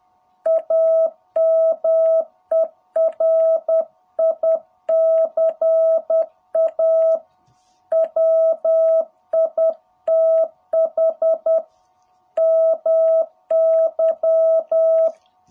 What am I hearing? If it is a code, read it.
Morse code, heard as a keyed tone, about 10 wpm, 660 Hz. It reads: AMERICA WITH MY
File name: jh8cbh / 2021cw